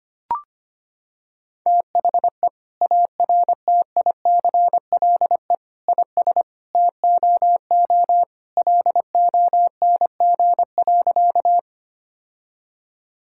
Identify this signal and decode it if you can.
Morse code, heard as a keyed tone, about 25 wpm, 700 Hz. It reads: THE ARTICLE IS TOO LONG.
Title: Sentences from Top 1000 Words - Review Part 5 - 25wpm